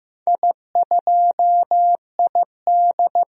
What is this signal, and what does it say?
Morse code, heard as a keyed tone, about 15 wpm, 700 Hz. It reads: I2ID
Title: DXCC-100 Call Signs - 4 Characters - Repeated 3x - 15wpm